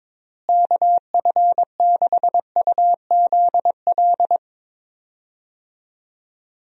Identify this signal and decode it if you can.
Morse code, heard as a keyed tone, about 22 wpm, 700 Hz. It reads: KF6UZL
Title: USA Call Signs - 6 Characters - Repeated 2x - 22wpm